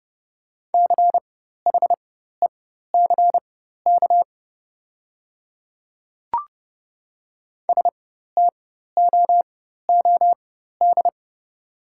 Morse code, heard as a keyed tone, about 30 wpm, spaced as for 14 wpm, 700 Hz. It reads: CHECK STOOD
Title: 4x Character Spacing - Top 500 Words - 30wpm